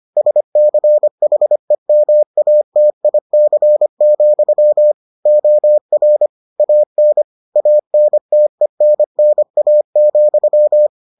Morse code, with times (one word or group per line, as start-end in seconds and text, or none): 0.2-4.9 SCHEMATIC,
5.3-6.3 OR
6.6-7.2 AN
7.6-10.9 ANTENNA,